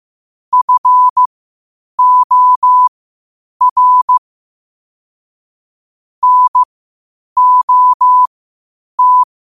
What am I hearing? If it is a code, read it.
Morse code, heard as a keyed tone, about 15 wpm, spaced as for 8 wpm, 1000 Hz. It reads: FOR NOT
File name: qotdfiles / 8wpm